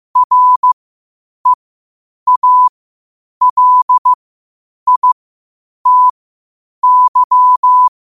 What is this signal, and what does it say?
Morse code, heard as a keyed tone, about 15 wpm, spaced as for 8 wpm, 1000 Hz. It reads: REALITY
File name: qotdfiles / 8wpm